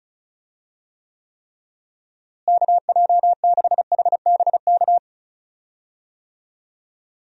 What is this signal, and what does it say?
Morse code, heard as a keyed tone, about 36 wpm, 700 Hz. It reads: KJ6HBK
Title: USA Call Signs - 6 Characters - Repeated 3x - 35wpm